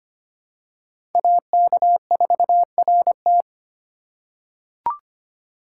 Morse code, heard as a keyed tone, about 25 wpm, 700 Hz. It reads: AK4RT
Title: USA Call Signs - 3x Character Spacing - 25wpm